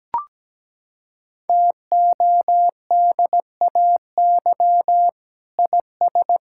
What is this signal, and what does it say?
Morse code, heard as a keyed tone, about 17 wpm, 700 Hz. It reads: TODAY IS
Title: Sentences from Top 1600 Words 17wpm